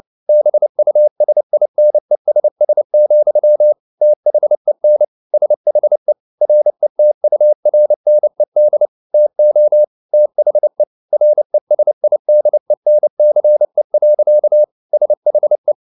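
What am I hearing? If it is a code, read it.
Morse code, heard as a keyed tone, about 29 wpm, 600 Hz. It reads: BUSINESS, THEN SHE RETURNED TO THE RESIDENCE. SHE
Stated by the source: Morse code at 29 wpm, sending Ken Follett's novel Never in English, from CW med Gnister.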